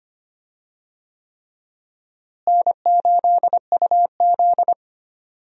Morse code, heard as a keyed tone, about 25 wpm, 700 Hz. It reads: N8UZ
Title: USA Call Signs - 4 Characters - Repeated 3x - 25wpm